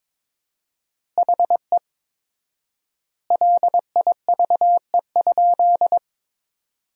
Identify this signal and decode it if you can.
Morse code, heard as a keyed tone, about 21 wpm, 700 Hz. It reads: HE LIVE?
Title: 4x Word-Spacing - Sentences from Top 200 Words - 22wpm